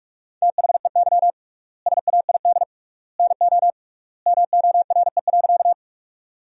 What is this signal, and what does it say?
Morse code, heard as a keyed tone, about 46 wpm, 700 Hz. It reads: THEY SAID NO MORE.
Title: 3x Word-Spacing - Sentences from Top 100 Words - 45wpm